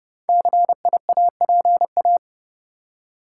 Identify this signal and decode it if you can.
Morse code, heard as a keyed tone, about 30 wpm, 700 Hz. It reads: CIAPA
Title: Code Groups - 5 Letters - 30wpm